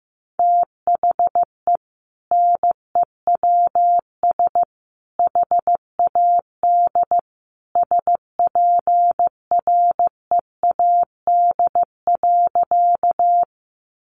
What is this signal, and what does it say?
Morse code, heard as a keyed tone, about 15 wpm, 700 Hz. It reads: THE NEWS HAD SPREAD.
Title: Sentences from Top 1200 Words 15wpm